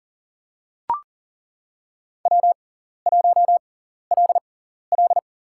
Morse code, heard as a keyed tone, about 42 wpm, spaced as for 14 wpm, 700 Hz. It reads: W1LL
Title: USA Call Signs - 6x Character Spacing - 40wpm